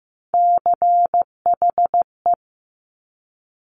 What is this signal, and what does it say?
Morse code, heard as a keyed tone, about 15 wpm, 690 Hz. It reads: CHE
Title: Top Three Letter Combinations in Words 15wpm